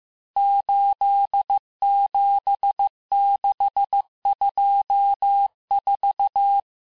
Morse code, heard as a keyed tone, about 15 wpm, 800 Hz. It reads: 87624
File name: cours-du-lundi files / cours-1585-texte-chiffre-vitesse-14mpm